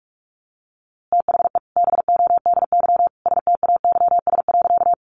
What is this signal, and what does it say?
Morse code, heard as a keyed tone, about 44 wpm, 700 Hz. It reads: THE BODY STAYS.
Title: A Sound Of Thunder 45wpm